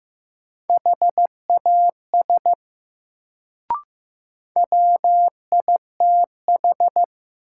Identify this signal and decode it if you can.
Morse code, heard as a keyed tone, about 15 wpm, 700 Hz. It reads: HAS WITH
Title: Sentences from Top 100 Words - New Word Drill - 15wpm